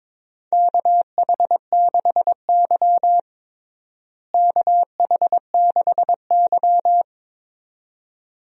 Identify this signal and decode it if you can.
Morse code, heard as a keyed tone, about 22 wpm, 700 Hz. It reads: KH6Y KH6Y
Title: DXCC-100 Call Signs - 4 Characters - Repeated 2x - 22wpm